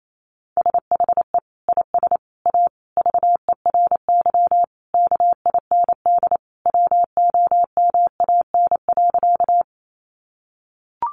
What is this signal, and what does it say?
Morse code, heard as a keyed tone, about 28 wpm, 700 Hz. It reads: SHE IS A VERY KIND WOMAN.
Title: Sentences from Top 1100 Words 28wpm